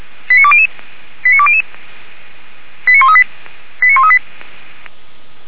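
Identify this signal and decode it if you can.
Five-tone selective call (ZVEI), heard as quick runs of five tones, tones of about 70 ms. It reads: 89200 89200 89138 89138
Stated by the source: ZVEI five-tone selective call received by SDR, from the Feuerwehr Orlamünde fire brigade alerting archive.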